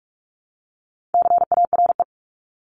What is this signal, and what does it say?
Morse code, heard as a keyed tone, about 44 wpm, 700 Hz. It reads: CARE
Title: Top 500 Words 45wpm